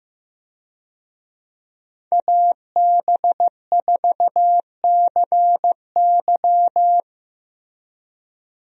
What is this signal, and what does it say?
Morse code, heard as a keyed tone, about 15 wpm, 700 Hz. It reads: AB4CY
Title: USA Call Signs - 5 Characters - Repeated 2x - 15wpm